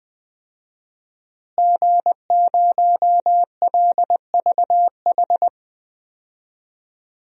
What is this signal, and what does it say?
Morse code, heard as a keyed tone, about 20 wpm, 700 Hz. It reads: G0LVH
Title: DXCC-100 Call Signs - 5 Characters - Repeated 2x - 20wpm